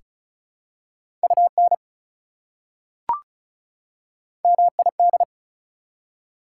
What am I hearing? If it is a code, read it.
Morse code, heard as a keyed tone, about 36 wpm, 700 Hz. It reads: UN MID
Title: Prefixes and Suffixes 35wpm